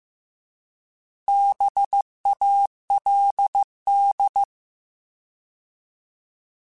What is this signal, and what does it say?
Morse code, heard as a keyed tone, about 15 wpm, 800 Hz. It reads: BALD